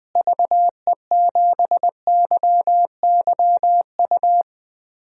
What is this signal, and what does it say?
Morse code, heard as a keyed tone, about 20 wpm, 700 Hz. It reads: VE7YYU
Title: DXCC-100 Call Signs - 6 Characters - Repeated 2x - 20wpm